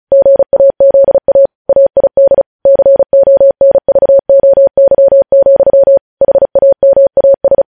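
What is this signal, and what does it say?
Morse code, heard as a keyed tone, about 36 wpm, 550 Hz. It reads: GAZA AID CONVOY, HAMAS
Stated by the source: News headlines sent in Morse at speed 35.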